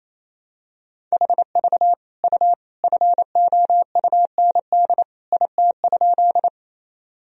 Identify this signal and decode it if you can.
Morse code, heard as a keyed tone, about 28 wpm, 700 Hz. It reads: HV U FOUND IT?
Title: Sentences from Top 300 Words and Abbreviations 28wpm